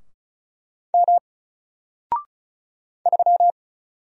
Morse code, heard as a keyed tone, about 36 wpm, 700 Hz. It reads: M 3
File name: Single-Letter-Number-35wpm